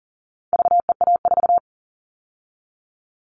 Morse code, heard as a keyed tone, about 40 wpm, 700 Hz. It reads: VEA4
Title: All Cut Numbers - 4-Digit 40wpm